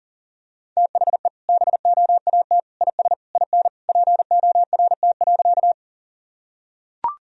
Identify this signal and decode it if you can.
Morse code, heard as a keyed tone, about 40 wpm, 700 Hz. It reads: THE BOAT IS IN PORT.
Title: Sentences from Top 1000 Words - Review Part 1 - 40wpm